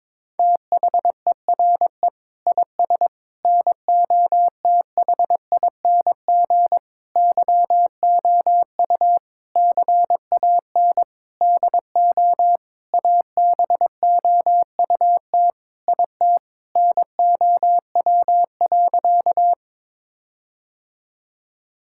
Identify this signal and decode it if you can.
Morse code, heard as a keyed tone, about 22 wpm, 700 Hz. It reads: THERE IS NOTHING YOU CAN DO ABOUT IT NOW.
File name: Sentences with 7-Word Phrases - 22wpm